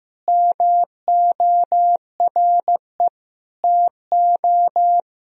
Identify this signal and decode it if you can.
Morse code, heard as a keyed tone, about 15 wpm, 700 Hz. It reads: MORE TO
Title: Sentences from Top 1000 Words - Review Part 4 - 15wpm